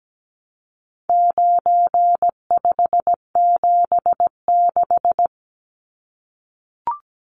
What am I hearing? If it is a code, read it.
Morse code, heard as a keyed tone, about 17 wpm, 700 Hz. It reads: 9576
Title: Numbers - 4-Digit 17wpm